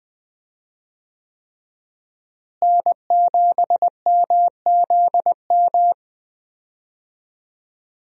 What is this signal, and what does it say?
Morse code, heard as a keyed tone, about 20 wpm, 700 Hz. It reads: N7MZM
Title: USA Call Signs - 5 Characters - Repeated 3x - 20wpm